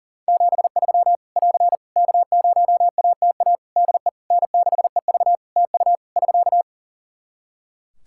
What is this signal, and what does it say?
Morse code, heard as a keyed tone, about 40 wpm, 700 Hz. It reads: 73 <AR> K0ATA DE N6EV TU <SK>
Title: Practice QSO - Exchange 4 - Ending - 40wpm